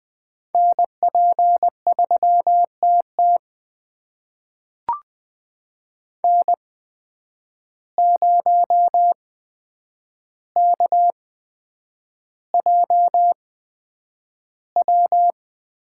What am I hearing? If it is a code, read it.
Morse code, heard as a keyed tone, about 20 wpm, spaced as for 9 wpm, 700 Hz. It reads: NP3TT N0KJW
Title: USA Call Signs - 8x Character Spacing - 20wpm